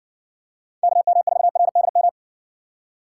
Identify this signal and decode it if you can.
Morse code, heard as a keyed tone, about 58 wpm, 700 Hz. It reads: KM4RDM